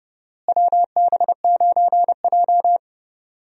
Morse code, heard as a keyed tone, about 30 wpm, 700 Hz. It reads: WB9J
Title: USA Call Signs - 4 Characters - Repeated 3x - 30wpm